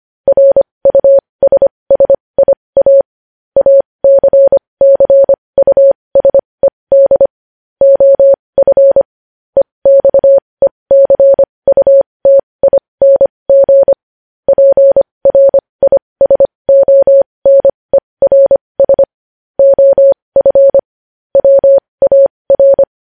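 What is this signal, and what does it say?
Morse code, heard as a keyed tone, about 25 wpm, spaced as for 20 wpm, 550 Hz. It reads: RUSSIA ACCUSED OF EXECUTING PRISONERS OF WAR
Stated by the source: Morse news headlines practice at 20 wpm.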